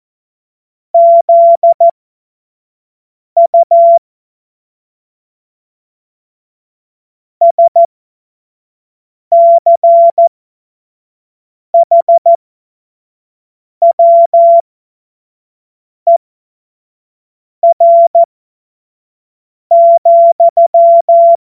Morse code, heard as a keyed tone, about 14 wpm, spaced as for 5 wpm, 680 Hz. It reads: ZU SCHWER,
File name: Morsekurs / Witze / Witz 30. Dezember 2025 25 F70